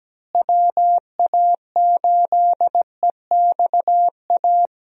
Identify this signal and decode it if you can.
Morse code, heard as a keyed tone, about 17 wpm, 700 Hz. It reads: WA8EXA
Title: USA Call Signs - 6 Characters - Repeated 2x - 17wpm